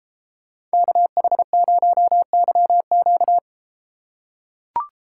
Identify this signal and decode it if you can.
Morse code, heard as a keyed tone, about 31 wpm, 700 Hz. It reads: KH0YQ